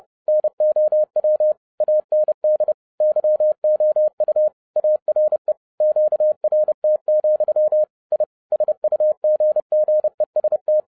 Morse code, heard as a keyed tone, about 30 wpm, 610 Hz. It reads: NOW AND YOU ARE QRT, I SUGGEST
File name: 2012-01-mp3 / n12-I DIDNT MISS THE SOLAR CYCLE 24-30wpm